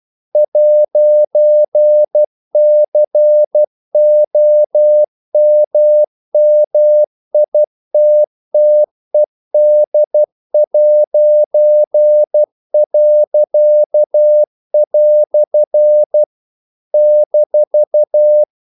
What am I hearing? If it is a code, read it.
Morse code, heard as a keyed tone, about 12 wpm, 600 Hz. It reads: 'COMMITTED'." -
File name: CW / Practice / EnglishQuotations0112